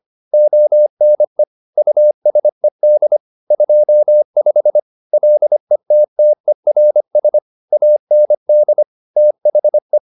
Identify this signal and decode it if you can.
Morse code, heard as a keyed tone, about 25 wpm, 600 Hz. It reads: ONE USED 25 LETTERS AND THE